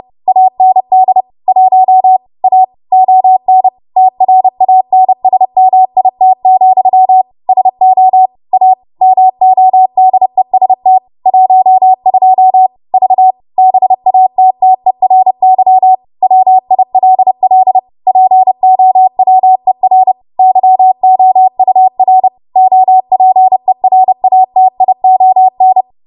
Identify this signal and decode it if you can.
Morse code, heard as a keyed tone, about 30 wpm, 750 Hz. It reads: AND 1 A ON TRANSMIT, SO A MODEST 12 V BATTERY WILL POWER YOUR OPERATION